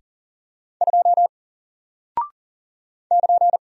Morse code, heard as a keyed tone, about 40 wpm, 700 Hz. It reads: 2 <KN>
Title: Single Character (Letter:Number:Punctuation:Prosign) Alt 40wpm